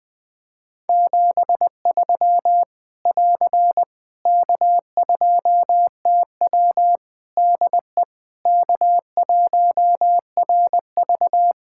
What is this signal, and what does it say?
Morse code, heard as a keyed tone, about 20 wpm, 700 Hz. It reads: 73 <AR> K2TW DE K1RV
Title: K1USN SST QSO Practice 20wpm